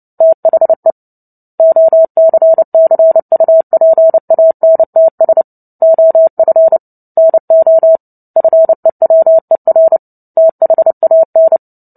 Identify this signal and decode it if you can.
Morse code, heard as a keyed tone, about 30 wpm, 650 Hz. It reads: THE OCCUPANTS OF NO FEWER THAN
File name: Book Two / 03-Chapter 3 - 29 WPM